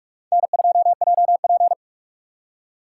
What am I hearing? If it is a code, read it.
Morse code, heard as a keyed tone, about 38 wpm, 700 Hz. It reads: N2JP